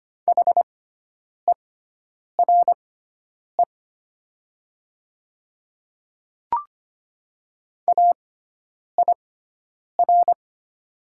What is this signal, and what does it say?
Morse code, heard as a keyed tone, about 25 wpm, spaced as for 9 wpm, 700 Hz. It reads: HERE AIR
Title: 6x Character Spacing - Top 200 Words - 25wpm